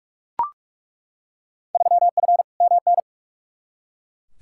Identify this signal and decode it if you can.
Morse code, heard as a keyed tone, about 42 wpm, 700 Hz. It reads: 3F MN
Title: Field Day Exchanges Encore - 45wpm